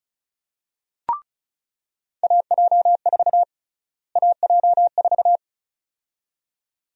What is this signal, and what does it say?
Morse code, heard as a keyed tone, about 34 wpm, 700 Hz. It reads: AJ4 AJ4
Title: USA Call Sign Prefixes - Repeated 2x - 35wpm